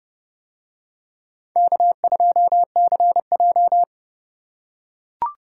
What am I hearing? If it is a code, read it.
Morse code, heard as a keyed tone, about 30 wpm, 700 Hz. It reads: K2CJ